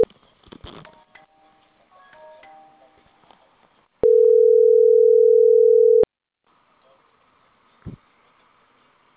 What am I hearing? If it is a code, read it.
Ringback tone (North America), 440 + 480 Hz.